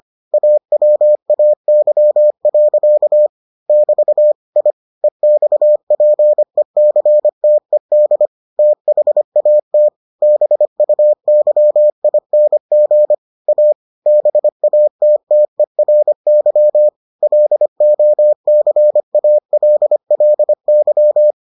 Morse code, heard as a keyed tone, about 25 wpm, 600 Hz. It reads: AWAY. <BT> I EXPECTED THAT BUYING A BATTERY LOCALLY